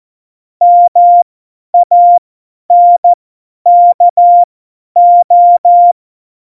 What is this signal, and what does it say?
Morse code, heard as a keyed tone, about 14 wpm, spaced as for 10 wpm, 700 Hz. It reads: MANKO
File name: Morsekurs / Witze / Witz 14 Juli 2020, 50 F70